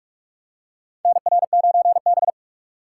Morse code, heard as a keyed tone, about 46 wpm, 700 Hz. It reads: NR9B
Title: USA Call Signs - 4 Characters - 4x Character Spacing - 45wpm